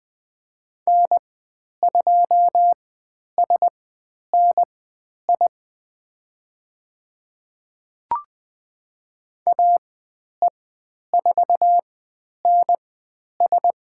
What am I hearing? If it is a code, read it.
Morse code, heard as a keyed tone, about 20 wpm, spaced as for 10 wpm, 700 Hz. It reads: N2SNI AE4NS
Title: 109 Lesson 18g - Letter m - Call Signs 20wpm